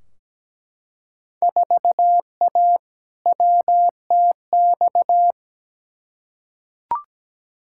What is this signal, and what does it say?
Morse code, heard as a keyed tone, about 17 wpm, 700 Hz. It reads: 4A WTX
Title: Field Day Exchanges Encore - 17wpm